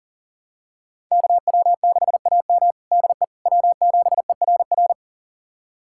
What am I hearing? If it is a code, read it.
Morse code, heard as a keyed tone, about 40 wpm, 700 Hz. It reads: KW6AM DE W7ERR
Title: USA Callsign DE Callsign - 5 Characters - 40wpm